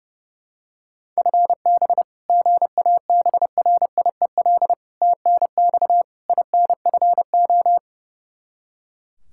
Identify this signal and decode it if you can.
Morse code, heard as a keyed tone, about 32 wpm, 700 Hz. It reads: FB GABRIEL TNX INFO
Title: Practice QSO - Exchange 3 - More Info - 30wpm